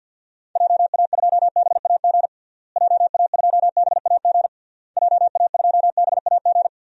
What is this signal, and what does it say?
Morse code, heard as a keyed tone, about 49 wpm, 700 Hz. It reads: JA2BAG JA2BAG JA2BAG
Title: DXCC-100 Call Signs - 6 Characters - Repeated 3x - 50wpm